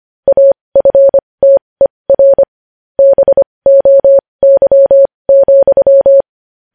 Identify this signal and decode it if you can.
Morse code, heard as a keyed tone, about 25 wpm, spaced as for 20 wpm, 550 Hz. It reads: AFTER BOY,
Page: News headlines in morse code at 20 wpm